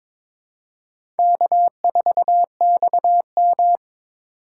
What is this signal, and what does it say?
Morse code, heard as a keyed tone, about 22 wpm, 700 Hz. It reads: K4XM
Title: USA Call Signs - 4x Character Spacing - 22wpm